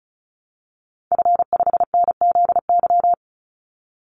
Morse code, heard as a keyed tone, about 34 wpm, 700 Hz. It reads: F5NZY
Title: Call Signs Repeated 3x - 35wpm